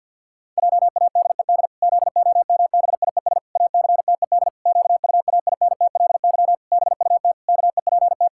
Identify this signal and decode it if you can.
Morse code, heard as a keyed tone, about 52 wpm, 700 Hz. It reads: JADED ZOMBIES ACTED QUAINTLY BUT KEPT
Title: Pangram Challenge v2 - 50wpm